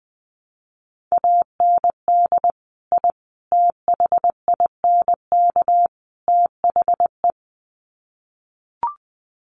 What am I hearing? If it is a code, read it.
Morse code, heard as a keyed tone, about 20 wpm, 700 Hz. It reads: AND I THINK THE
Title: Sets of 4 Words 20wpm